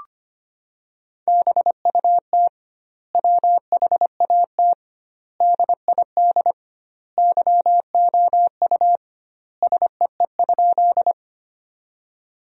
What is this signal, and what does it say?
Morse code, heard as a keyed tone, about 25 wpm, 700 Hz. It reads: BUT WHAT DID YOU SEE?
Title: 2x Word-Spacing - Sentences from Top 100 Words - 25wpm